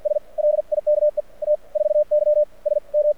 Morse code, heard as a keyed tone, about 53 wpm, spaced as for 43 wpm, 600 Hz. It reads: S LIKE T 4Y SM
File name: archived / CRSnet